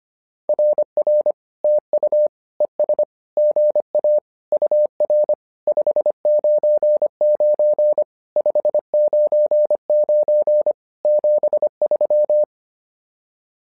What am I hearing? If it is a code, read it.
Morse code, heard as a keyed tone, about 25 wpm, 600 Hz. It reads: RR TU ES GA UR 599 599 73
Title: SOTA QSO Practice 25wpm